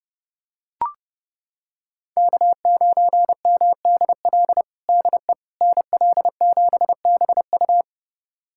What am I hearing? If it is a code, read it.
Morse code, heard as a keyed tone, about 30 wpm, 700 Hz. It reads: K9MDL DE NL7BU
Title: USA Callsign DE Callsign - 5 Characters - 30wpm